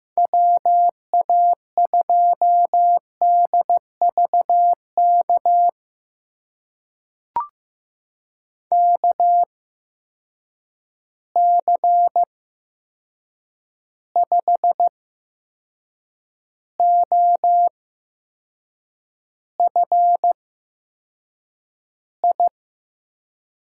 Morse code, heard as a keyed tone, about 15 wpm, spaced as for 6 wpm, 700 Hz. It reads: WA2DVK KC5OFI
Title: USA Call Signs - 6 Characters - 8x Character Spacing - 15wpm